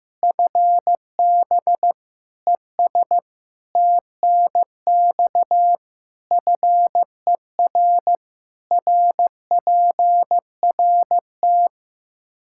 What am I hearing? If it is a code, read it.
Morse code, heard as a keyed tone, about 15 wpm, 700 Hz. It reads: FB ES TNX FER RPRT
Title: Practice QSO - Exchange 2 - Info - 15wpm